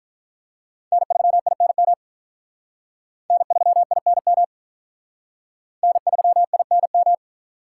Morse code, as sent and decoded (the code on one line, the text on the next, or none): -. ...-- .. -. -- / -. ...-- .. -. -- / -. ...-- .. -. --
N3INM N3INM N3INM